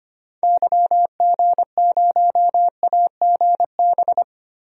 Morse code, heard as a keyed tone, about 25 wpm, 700 Hz. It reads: YG0AGB
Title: DXCC-100 Call Signs - 6 Characters - 25wpm